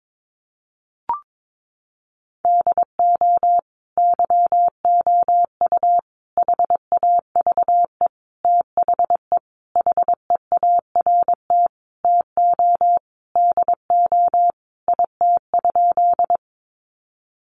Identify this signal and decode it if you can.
Morse code, heard as a keyed tone, about 22 wpm, 700 Hz. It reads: DO YOU HAVE THE HEART TO DO IT?
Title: Sentences from Top 1200 Words 22wpm